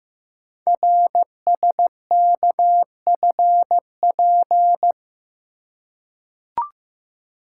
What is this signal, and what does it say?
Morse code, heard as a keyed tone, about 15 wpm, 700 Hz. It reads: RSKFP
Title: Code Groups - 5 Letters - 15wpm